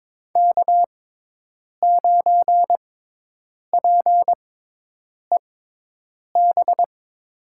Morse code, heard as a keyed tone, about 22 wpm, spaced as for 8 wpm, 700 Hz. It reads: K9PEB
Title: USA Call Signs - 5 Characters - 6x Character Spacing - 22wpm